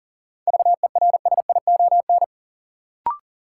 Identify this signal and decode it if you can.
Morse code, heard as a keyed tone, about 40 wpm, 700 Hz. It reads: VERSION
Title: Sentences from Top 1400 Words - New Word Drill Alt - 40wpm